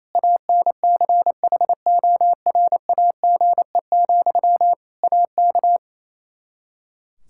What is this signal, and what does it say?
Morse code, heard as a keyed tone, about 28 wpm, 700 Hz. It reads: ANCHORAGE, AK
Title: K1USN SST QSO Practice 28wpm